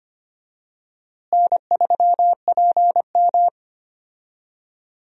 Morse code, heard as a keyed tone, about 25 wpm, 700 Hz. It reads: N3PM
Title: USA Call Signs - 4 Characters - Repeated 2x - 25wpm